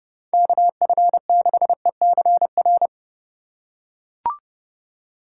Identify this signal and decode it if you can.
Morse code, heard as a keyed tone, about 30 wpm, 700 Hz. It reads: KF6ECR
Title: USA Call Signs - 6 Characters - 2x Character Spacing - 30wpm